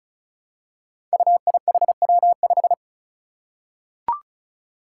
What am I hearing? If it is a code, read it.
Morse code, heard as a keyed tone, about 34 wpm, 700 Hz. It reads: UIHW5
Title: Five Challenging Letter-Number Combinations 35wpm